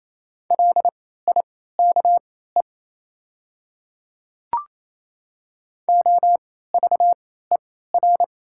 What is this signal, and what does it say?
Morse code, heard as a keyed tone, about 28 wpm, spaced as for 16 wpm, 700 Hz. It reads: LIKE OVER
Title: 3x Character Spacing - Top 100 Words - 28wpm